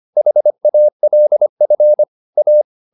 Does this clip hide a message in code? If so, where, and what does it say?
Yes: Morse code from 0.2 to 2.6 s, heard as a keyed tone, about 25 wpm, 600 Hz. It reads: HALF A